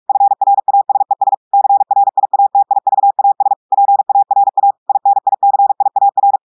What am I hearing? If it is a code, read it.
Morse code, heard as a keyed tone, about 46 wpm, 800 Hz. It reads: FRASES CRIATIVAS PARA INICIAR
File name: audios / portugais / portugais texte en clair 2 vitesse 45 mots minutes